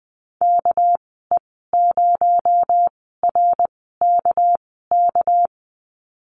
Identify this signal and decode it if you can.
Morse code, heard as a keyed tone, about 20 wpm, spaced as for 14 wpm, 700 Hz. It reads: KE0RKK
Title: 245 Lesson 39g - Letter z - Call Signs 20wpm